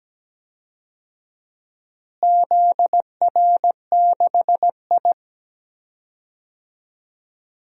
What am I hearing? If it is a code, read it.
Morse code, heard as a keyed tone, about 17 wpm, 700 Hz. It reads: ZR6I
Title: DXCC-100 Call Signs - 4 Characters - Repeated 3x - 17wpm